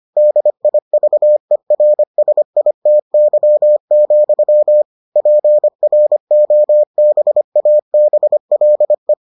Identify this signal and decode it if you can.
Morse code, heard as a keyed tone, about 25 wpm, 600 Hz. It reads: DIVERSITY, PROBABLE